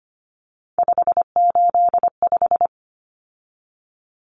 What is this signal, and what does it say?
Morse code, heard as a keyed tone, about 25 wpm, 700 Hz. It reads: 585
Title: Numbers - 3-Digit 25wpm